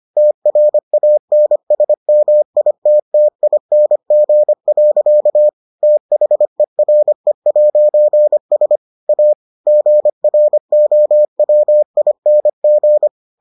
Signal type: Morse code

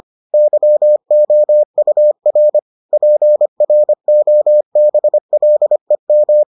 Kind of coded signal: Morse code

